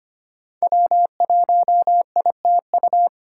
Morse code, heard as a keyed tone, about 25 wpm, 700 Hz. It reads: W1ITU
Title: USA Call Signs - 2x Character Spacing - 25wpm